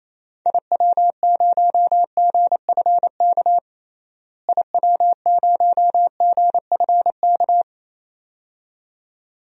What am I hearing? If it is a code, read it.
Morse code, heard as a keyed tone, about 28 wpm, 700 Hz. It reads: IW0GFK IW0GFK
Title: DXCC-100 Call Signs - 6 Characters - Repeated 2x - 28wpm